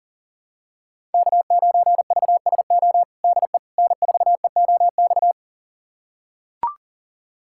Morse code, heard as a keyed tone, about 40 wpm, 700 Hz. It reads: K9VSO DE N4EOX
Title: USA Callsign DE Callsign - 5 Characters - 40wpm